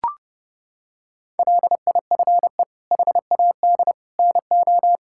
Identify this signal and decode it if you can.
Morse code, heard as a keyed tone, about 30 wpm, 700 Hz. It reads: LIFE HAD NO
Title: Sentences from Top 1000 Words - Review Part 6 - 30wpm